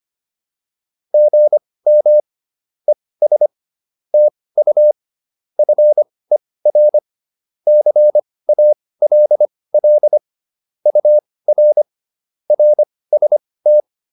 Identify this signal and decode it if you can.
Morse code, heard as a keyed tone, about 25 wpm, spaced as for 18 wpm, 600 Hz. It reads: GM ES TU FER CALL UR RST